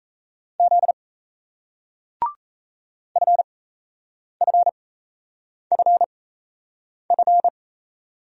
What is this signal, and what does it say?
Morse code, heard as a keyed tone, about 38 wpm, spaced as for 8 wpm, 700 Hz. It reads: Z FFFF